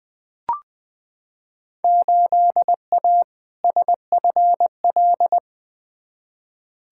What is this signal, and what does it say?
Morse code, heard as a keyed tone, about 20 wpm, 700 Hz. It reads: 8A SFL
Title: Field Day Exchanges Encore - 20wpm